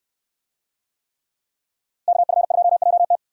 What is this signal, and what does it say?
Morse code, heard as a keyed tone, about 69 wpm, 700 Hz. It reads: KF2JT